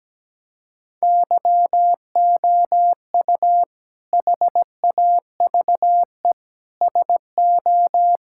Morse code, heard as a keyed tone, about 17 wpm, 700 Hz. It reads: YOU HAVE SO
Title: Sentences from Top 1600 Words 17wpm